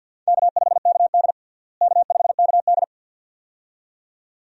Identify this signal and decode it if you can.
Morse code, heard as a keyed tone, about 52 wpm, 700 Hz. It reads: K5KD K5KD
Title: USA Call Signs - 4 Characters - Repeated 2x - 50wpm